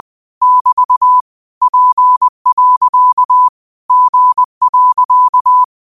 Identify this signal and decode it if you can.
Morse code, heard as a keyed tone, about 20 wpm, 1000 Hz. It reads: <BT> P. G.